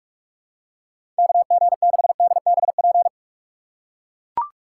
Morse code, heard as a keyed tone, about 42 wpm, 700 Hz. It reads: KG6DBP
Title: USA Call Signs - 6 Characters - 3x Character Spacing - 45wpm